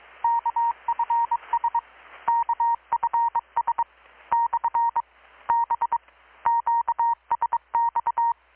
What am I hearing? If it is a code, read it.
Morse code, heard as a keyed tone, about 21 wpm, 930 Hz. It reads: KFS KFS / B QSX
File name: blog / swl / 6-abril-2019-12695-USB-KPS